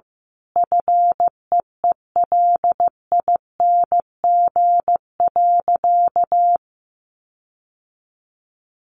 Morse code, heard as a keyed tone, about 15 wpm, 700 Hz. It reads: FEELING.
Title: Sentences from Top 1400 Words 15wpm